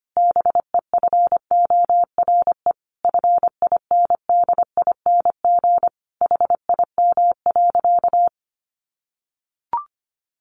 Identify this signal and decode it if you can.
Morse code, heard as a keyed tone, about 25 wpm, 700 Hz. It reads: BEFORE FINDING HIM.
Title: Sentences from Top 1100 Words 25wpm